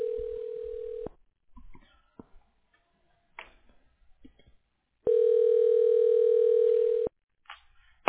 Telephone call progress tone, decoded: ringback tone